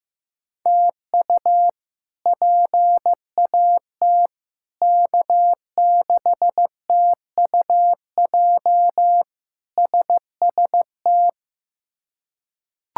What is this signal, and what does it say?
Morse code, heard as a keyed tone, about 15 wpm, 700 Hz. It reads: TU PAT K6TUJ SST
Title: K1USN SST QSO Practice - 15wpm